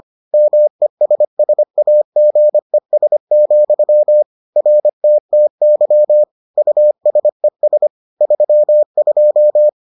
Morse code, heard as a keyed tone, about 25 wpm, 600 Hz. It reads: MESSAGES, RTTY USES 32